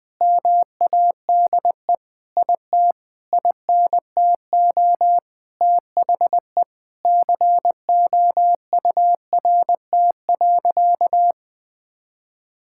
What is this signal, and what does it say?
Morse code, heard as a keyed tone, about 20 wpm, 700 Hz. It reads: MADE IT INTO THE COURT.
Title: Sentences from Top 1600 Words 20wpm